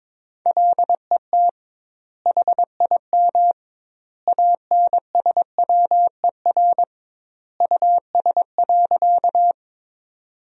Morse code, heard as a keyed tone, about 22 wpm, 700 Hz. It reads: LET HIM ANSWER US.
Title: 2x Word-Spacing - Sentences from Top 300 Words - 22wpm